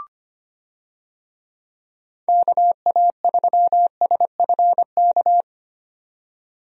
Morse code, heard as a keyed tone, about 25 wpm, 700 Hz. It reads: KA3SFK
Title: USA Call Signs - 6 Characters - 25wpm